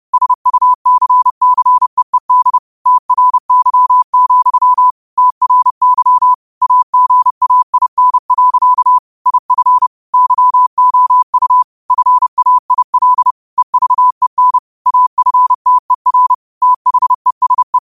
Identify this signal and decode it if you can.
Morse code, heard as a keyed tone, about 30 wpm, 1000 Hz. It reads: SUCCEED TRY, TRY AGAIN. IF YOU FAIL EVEN AFTER THESE
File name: qotdfiles / 30wpm